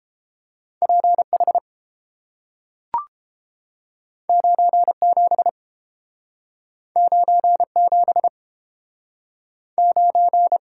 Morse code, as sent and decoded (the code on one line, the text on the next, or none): .--. / .... / ----. / --... / ----. / --... / ----.
P H 9 7 9 7 9